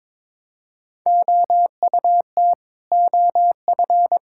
Morse code, heard as a keyed tone, about 22 wpm, 700 Hz. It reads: OUT OF
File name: practice / 22wpm / Sets of 7 Words - 22wpm